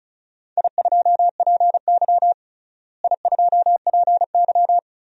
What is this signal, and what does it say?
Morse code, heard as a keyed tone, about 36 wpm, 700 Hz. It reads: I2PY I2PY
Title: DXCC-100 Call Signs - 4 Characters - Repeated 3x - 35wpm